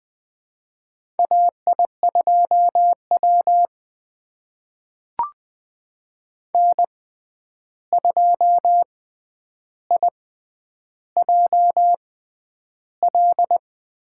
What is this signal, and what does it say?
Morse code, heard as a keyed tone, about 20 wpm, spaced as for 10 wpm, 700 Hz. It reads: AI2W N2IJL
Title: USA Call Signs - 6x Character Spacing - 20wpm